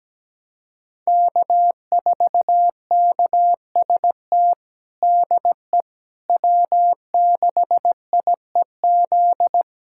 Morse code, heard as a keyed tone, about 17 wpm, 700 Hz. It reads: K4KST DE W6IEZ
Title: USA Callsign DE Callsign - 5 Characters - 17wpm